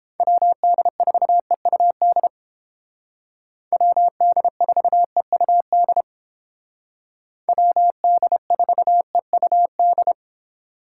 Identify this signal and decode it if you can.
Morse code, heard as a keyed tone, about 28 wpm, 700 Hz. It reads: WD4EUD WD4EUD WD4EUD